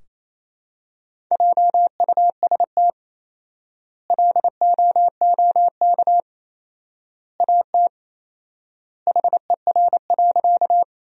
Morse code, heard as a keyed tone, about 28 wpm, 700 Hz. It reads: JUST LOOK AT HER.
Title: 4x Word-Spacing - Sentences from Top 200 Words - 28wpm